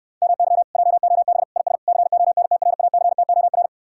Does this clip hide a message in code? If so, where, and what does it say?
Yes: Morse code from 0.2 to 3.7 s, heard as a keyed tone, about 67 wpm, 700 Hz. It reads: MY JOB IS CONTRACTOR